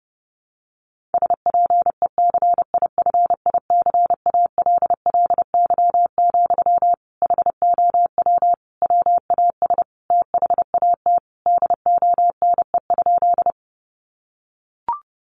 Morse code, heard as a keyed tone, about 30 wpm, 700 Hz. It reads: SPECIFICALLY, HOW WAS THAT DONE?
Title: Sentences from Top 1100 Words 30wpm